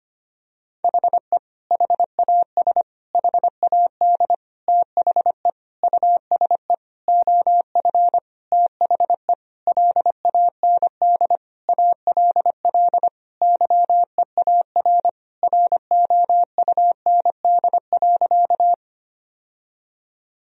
Morse code, heard as a keyed tone, about 25 wpm, 700 Hz. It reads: HE HAS HAD THE USE OF THE LAND ALL YEAR ROUND.
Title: Sentences from Top 1000 Words - Review Part 1 - 25wpm